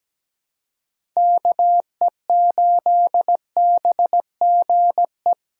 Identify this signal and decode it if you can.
Morse code, heard as a keyed tone, about 17 wpm, 700 Hz. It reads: KE8BGE
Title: USA Call Signs - 6 Characters - 6x Character Spacing - 17wpm